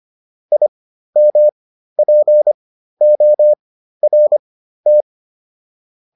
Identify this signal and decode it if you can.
Morse code, heard as a keyed tone, about 25 wpm, spaced as for 13 wpm, 600 Hz. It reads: IMPORT